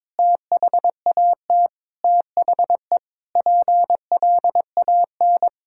Morse code, heard as a keyed tone, about 22 wpm, 700 Hz. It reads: THAT THE PLAN
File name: Sentences with 9-Word Phrases - 22wpm